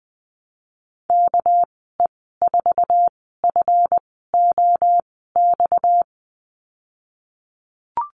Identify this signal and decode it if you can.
Morse code, heard as a keyed tone, about 20 wpm, spaced as for 15 wpm, 700 Hz. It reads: KE4FOX
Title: 238 Lesson 38g - Letter q - Call Signs 20wpm